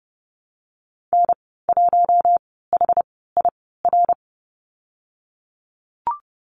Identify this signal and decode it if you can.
Morse code, heard as a keyed tone, about 30 wpm, spaced as for 17 wpm, 700 Hz. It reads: N1HIR
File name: Lesson 19g - Letter w - Call Signs 30wpm